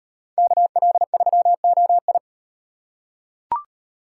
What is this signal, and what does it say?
Morse code, heard as a keyed tone, about 36 wpm, 700 Hz. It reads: KL3OI